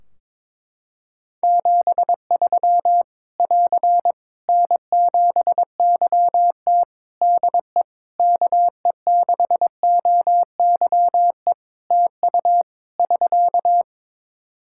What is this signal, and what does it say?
Morse code, heard as a keyed tone, about 21 wpm, 700 Hz. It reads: 73 <AR> N7YT DE KE6OYE TU <SK>
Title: Practice QSO - Exchange 4 - Ending - 22wpm